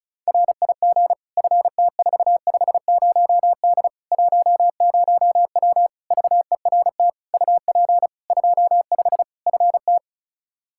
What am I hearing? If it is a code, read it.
Morse code, heard as a keyed tone, about 34 wpm, 700 Hz. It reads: RIG FT450D 10W VERT UP 25 FT
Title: Practice QSO - Exchange 2 - Info - 35wpm